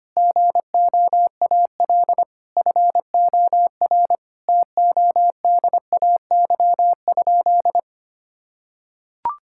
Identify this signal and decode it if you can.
Morse code, heard as a keyed tone, about 25 wpm, 700 Hz. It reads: GOAL FOR TODAY?
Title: Sentences from Top 1000 Words - Review Part 5 - 25wpm